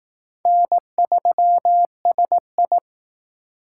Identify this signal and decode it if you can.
Morse code, heard as a keyed tone, about 18 wpm, 700 Hz. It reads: N3SI
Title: Speed Racing - USA Call Signs 15wpm